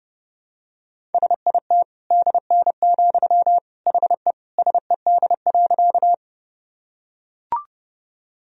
Morse code, heard as a keyed tone, about 32 wpm, 700 Hz. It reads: SIT DN, HE SED.
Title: Sentences from Top 500 Words and Abbreviations 30wpm